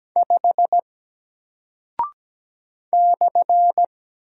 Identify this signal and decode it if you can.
Morse code, heard as a keyed tone, about 17 wpm, 700 Hz. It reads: 5 /
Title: Single Character (Letter:Number:Punctuation:Prosign) Alt 17wpm